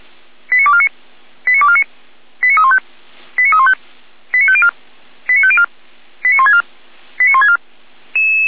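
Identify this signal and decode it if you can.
Five-tone selective call (ZVEI), heard as quick runs of five tones, tones of about 70 ms. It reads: 89249 89249 89315 89315 89504 89504 89164 89164
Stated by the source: ZVEI five-tone selective call received by SDR.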